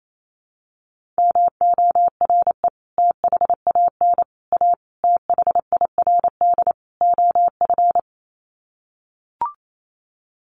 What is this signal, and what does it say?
Morse code, heard as a keyed tone, about 28 wpm, 700 Hz. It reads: MORE THAN A THIRD OF
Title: Sets of 5 Words Encore2 28wpm